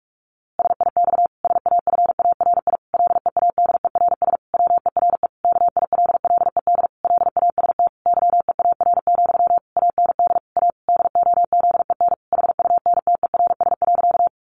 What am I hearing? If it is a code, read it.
Morse code, heard as a keyed tone, about 44 wpm, 700 Hz. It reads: SIX SAFARI LEADERS WERE KILLED LAST YEAR, AND A DOZEN HUNTERS.
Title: A Sound Of Thunder 45wpm